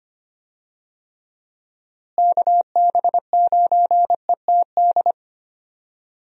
Morse code, heard as a keyed tone, about 25 wpm, 700 Hz. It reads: KB9ETD
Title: USA Call Signs - 6 Characters - Repeated 3x - 25wpm